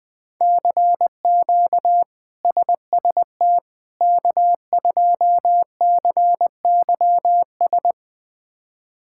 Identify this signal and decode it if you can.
Morse code, heard as a keyed tone, about 20 wpm, 700 Hz. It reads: CQ SST K2CYS
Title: K1USN SST QSO Practice - 20wpm